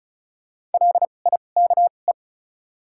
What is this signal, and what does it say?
Morse code, heard as a keyed tone, about 34 wpm, spaced as for 25 wpm, 700 Hz. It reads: LIKE
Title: 2x Character Spacing - Top 200 Words - 35wpm